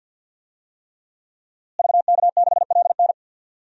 Morse code, heard as a keyed tone, about 49 wpm, 700 Hz. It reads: VK6LN
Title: DXCC-100 Call Signs - 5 Characters - Repeated 3x - 50wpm